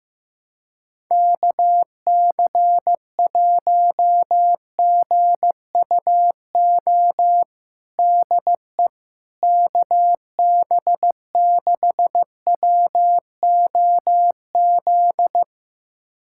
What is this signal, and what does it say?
Morse code, heard as a keyed tone, about 15 wpm, 700 Hz. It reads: KC1GUO DE KB6WOZ
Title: USA Callsign DE Callsign 15wpm